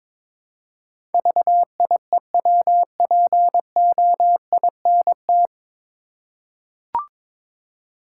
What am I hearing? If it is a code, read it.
Morse code, heard as a keyed tone, about 22 wpm, 700 Hz. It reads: VIEWPOINT
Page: Morse Code Ninja practice 22wpm